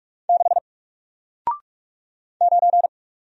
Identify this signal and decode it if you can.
Morse code, heard as a keyed tone, about 44 wpm, 700 Hz. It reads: 6 9
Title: Single Letter-Number Alt 45wpm